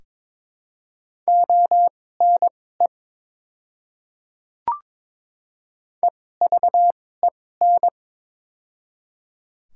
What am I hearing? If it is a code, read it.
Morse code, heard as a keyed tone, about 22 wpm, spaced as for 16 wpm, 700 Hz. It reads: ONE EVEN
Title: 2x Character Spacing - Top 300 Words - 22wpm